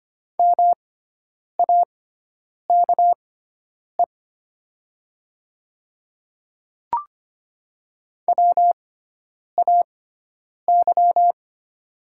Morse code, heard as a keyed tone, about 25 wpm, spaced as for 9 wpm, 700 Hz. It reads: MAKE WAY
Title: 6x Character Spacing - Top 100 Words - 25wpm